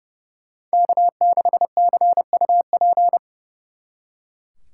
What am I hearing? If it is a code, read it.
Morse code, heard as a keyed tone, about 37 wpm, 700 Hz. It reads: K6CUP
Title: Speed Racing - USA 1x3 Call Signs 30wpm